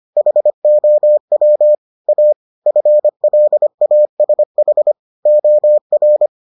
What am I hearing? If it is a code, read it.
Morse code, heard as a keyed tone, about 25 wpm, 600 Hz. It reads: HOW A FLASH OR